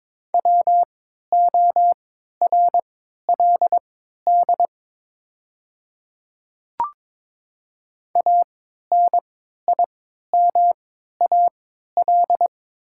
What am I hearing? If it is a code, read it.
Morse code, heard as a keyed tone, about 23 wpm, spaced as for 13 wpm, 700 Hz. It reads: WORLD ANIMAL
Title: 3x Character Spacing - Top 200 Words - 22wpm